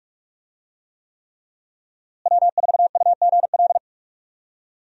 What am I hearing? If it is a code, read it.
Morse code, heard as a keyed tone, about 47 wpm, 700 Hz. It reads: W4UGL
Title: USA Call Signs - 5 Characters - Repeated 3x - 45wpm